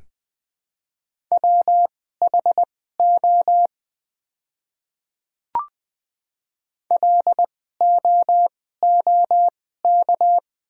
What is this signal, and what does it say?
Morse code, heard as a keyed tone, about 20 wpm, spaced as for 14 wpm, 690 Hz. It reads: WHO LOOK
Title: 2x Character Spacing - Top 300 Words - 20wpm